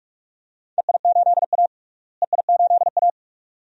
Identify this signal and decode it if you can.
Morse code, heard as a keyed tone, about 46 wpm, 700 Hz. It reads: EI8A EI8A
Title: DXCC-100 Call Signs - 4 Characters - Repeated 2x - 45wpm